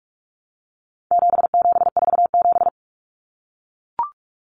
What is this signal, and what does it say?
Morse code, heard as a keyed tone, about 44 wpm, 700 Hz. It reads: 7747
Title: Common Cut Numbers - 4-Digit 45wpm